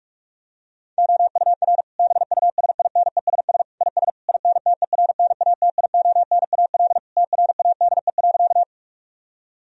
Morse code, heard as a keyed tone, about 46 wpm, 700 Hz. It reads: OUR BUSINESS IS INTERNATIONAL TRADE.
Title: Sentences from Top 1000 Words - Review Part 4 - 45wpm